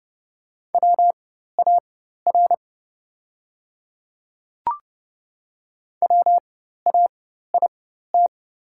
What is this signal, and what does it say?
Morse code, heard as a keyed tone, about 30 wpm, spaced as for 14 wpm, 700 Hz. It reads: WAR WAIT
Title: 4x Character Spacing - Top 500 Words - 30wpm